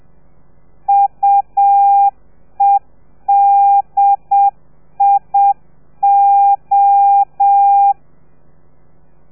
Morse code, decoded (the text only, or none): UEDIO